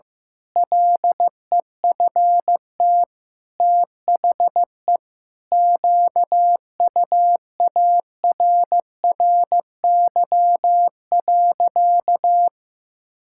Morse code, heard as a keyed tone, about 15 wpm, 700 Hz. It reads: LEFT THE QUARRY.
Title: Pangram Challenge v2 - 15wpm